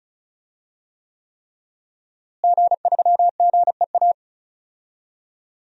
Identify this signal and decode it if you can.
Morse code, heard as a keyed tone, about 36 wpm, 700 Hz. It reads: G3GEA
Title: DXCC-100 Call Signs - 5 Characters - Repeated 3x - 35wpm